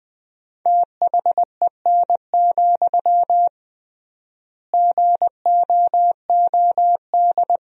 Morse code, heard as a keyed tone, about 20 wpm, 700 Hz. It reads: THEN, GOOD
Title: 3x Word-Spacing - Sentences from Top 300 Words - 20wpm